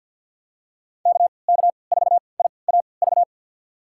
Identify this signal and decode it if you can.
Morse code, heard as a keyed tone, about 52 wpm, spaced as for 29 wpm, 700 Hz. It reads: KK4IAV
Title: USA Call Signs - 6 Characters - 3x Character Spacing - 50wpm